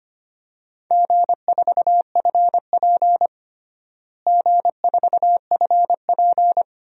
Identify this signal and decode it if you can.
Morse code, heard as a keyed tone, about 25 wpm, 700 Hz. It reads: G4FP G4FP
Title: DXCC-100 Call Signs - 4 Characters - Repeated 3x - 25wpm